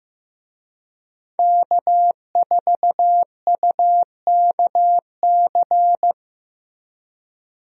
Morse code, heard as a keyed tone, about 15 wpm, 700 Hz. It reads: K4UKC
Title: USA Call Signs - 5 Characters - Repeated 3x - 15wpm